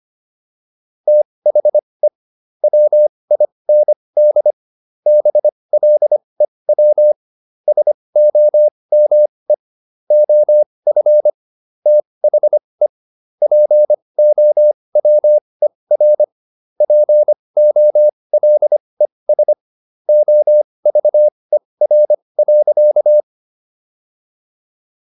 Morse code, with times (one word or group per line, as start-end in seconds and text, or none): 1.1-2.1 THE
2.6-4.5 WIND
5.1-7.1 BLEW
7.7-9.5 SOME
10.1-11.3 OF
11.9-12.9 THE
13.4-16.2 POWER
16.8-19.5 POLES
20.1-23.2 OVER.